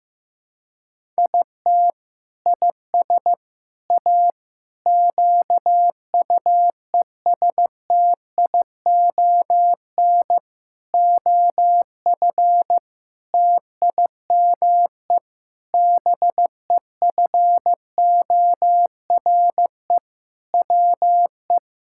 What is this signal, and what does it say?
Morse code, heard as a keyed tone, about 15 wpm, 700 Hz. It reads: IT IS A QUESTION OF TIME BEFORE WE
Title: Sentences from Top 1000 Words - Review Part 6 - 15wpm